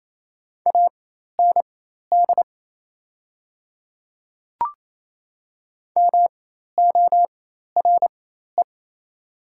Morse code, heard as a keyed tone, about 28 wpm, spaced as for 13 wpm, 700 Hz. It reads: AND MORE
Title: 4x Character Spacing - Top 100 Words - 28wpm